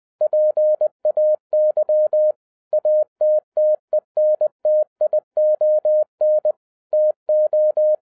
Morse code, heard as a keyed tone, about 20 wpm, 610 Hz. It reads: PAY ATTENTION TO
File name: 2011-10-mp3 / n13-TIPS FROM MY 34 YEARS OF EXPERIENCE 20wpm